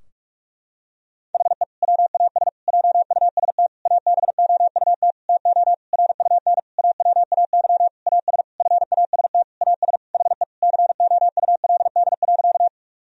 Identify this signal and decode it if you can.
Morse code, heard as a keyed tone, about 46 wpm, 700 Hz. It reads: HE WAS JUST ABOUT TO RUN AWAY AS FAST AS HE COULD.
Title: Sentences from Top 1000 Words - Review Part 2 - 45wpm